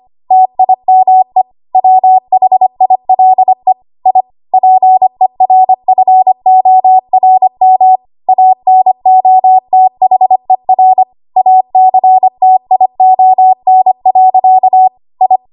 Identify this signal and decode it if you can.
Morse code, heard as a keyed tone, about 25 wpm, 750 Hz. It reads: TIME WHILE I PERFORM ANOTHER ACTION. I
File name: Archive / 25 WPM / 25WPM